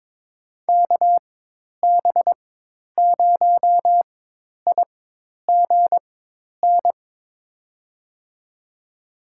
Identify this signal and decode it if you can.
Morse code, heard as a keyed tone, about 22 wpm, spaced as for 10 wpm, 700 Hz. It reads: KB0IGN
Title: USA Call Signs - 4x Character Spacing - 22wpm